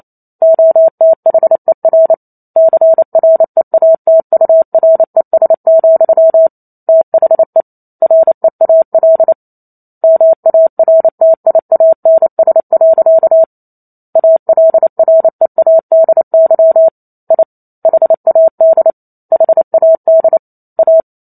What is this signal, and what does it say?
Morse code, heard as a keyed tone, about 28 wpm, 650 Hz. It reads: OTHER CREATURES, THE REAL MARTIANS. ALREADY I HAD HAD A